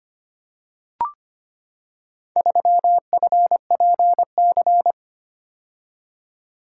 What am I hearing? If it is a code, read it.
Morse code, heard as a keyed tone, about 25 wpm, 700 Hz. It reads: 3FPC
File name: practice / 25wpm / Four Letter-Number Combinations - 25wpm